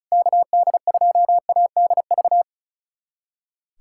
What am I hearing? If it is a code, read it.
Morse code, heard as a keyed tone, about 34 wpm, 700 Hz. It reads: KD2ADV
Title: USA Callsign DE Callsign - 6 Characters - 35wpm